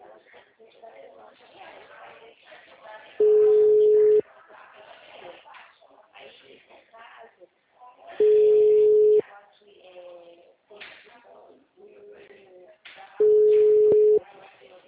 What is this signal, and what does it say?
Ringback tone (Europe, 425 Hz), with speech.